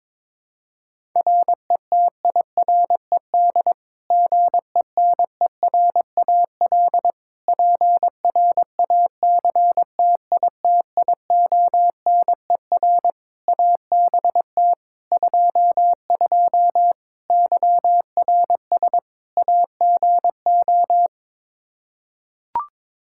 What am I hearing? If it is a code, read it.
Morse code, heard as a keyed tone, about 22 wpm, 700 Hz. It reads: RETIRED GENERAL PRACTITIONER ABT 22 YRS AGO
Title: Practice QSO - Exchange 3 - More Info - 22wpm